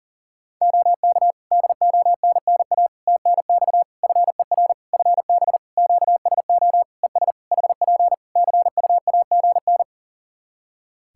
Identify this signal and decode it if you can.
Morse code, heard as a keyed tone, about 40 wpm, 700 Hz. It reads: OK DONNA TNX FER FB QSO ES HP CUAGN
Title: Practice QSO - Exchange 4 - Ending - 40wpm